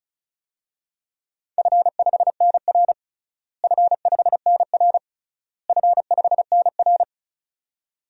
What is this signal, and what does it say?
Morse code, heard as a keyed tone, about 36 wpm, 700 Hz. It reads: F5NR F5NR F5NR
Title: DXCC-100 Call Signs - 4 Characters - Repeated 3x - 35wpm